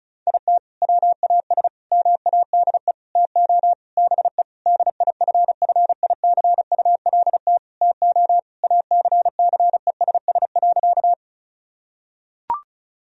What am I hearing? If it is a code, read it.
Morse code, heard as a keyed tone, about 34 wpm, 700 Hz. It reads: IT WAS MADE TO BE DIFFICULT TO ACCESS.
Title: Sentences from Top 1000 Words - Review Part 4 - 35wpm